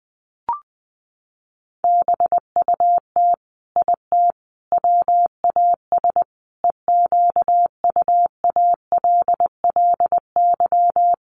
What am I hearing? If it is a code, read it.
Morse code, heard as a keyed tone, about 20 wpm, 700 Hz. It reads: BUT IT WAS EQUALLY
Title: Sentences from Top 1400 Words 20wpm